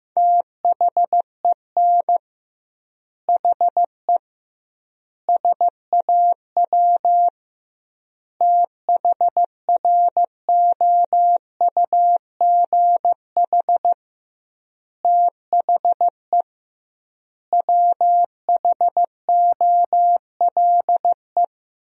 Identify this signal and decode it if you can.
Morse code, heard as a keyed tone, about 16 wpm, 700 Hz. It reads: THEN HE SAW THROUGH THE WHOLE
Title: 2x Word-Spacing - Sentences from Top 400 Words - 15wpm